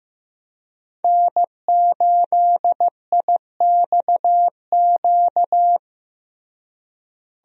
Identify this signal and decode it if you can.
Morse code, heard as a keyed tone, about 16 wpm, 700 Hz. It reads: N8IXQ
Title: USA Call Signs - 5 Characters - Repeated 2x - 15wpm